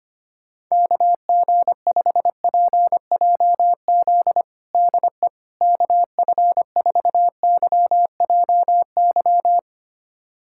Morse code, heard as a keyed tone, about 25 wpm, 700 Hz. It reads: KG5PJZ DE KF4YJY
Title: USA Callsign DE Callsign - 6 Characters - 25wpm